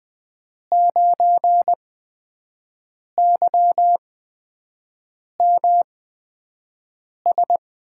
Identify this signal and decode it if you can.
Morse code, heard as a keyed tone, about 20 wpm, spaced as for 5 wpm, 700 Hz. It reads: 9YMS
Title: USA Call Signs - 8x Character Spacing - 20wpm